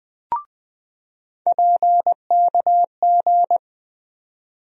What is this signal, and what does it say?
Morse code, heard as a keyed tone, about 20 wpm, 700 Hz. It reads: PKG
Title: USA Call Sign Postfixes - 3 Letters 20wpm